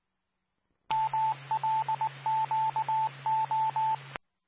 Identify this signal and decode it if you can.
Morse code, heard as a keyed tone, about 19 wpm, 880 Hz, with radio noise. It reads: MLQO
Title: Nassau and Suffolk County Amateur Repeaters